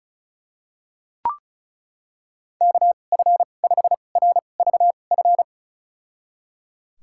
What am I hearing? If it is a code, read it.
Morse code, heard as a keyed tone, about 34 wpm, spaced as for 25 wpm, 700 Hz. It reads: KF5RVF
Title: USA Call Signs - 6 Characters - 2x Character Spacing - 35wpm